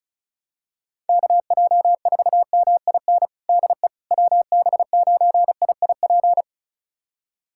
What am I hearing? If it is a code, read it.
Morse code, heard as a keyed tone, about 36 wpm, 700 Hz. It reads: KJ4MIN DE WB9IIP
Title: USA Callsign DE Callsign - 6 Characters - 35wpm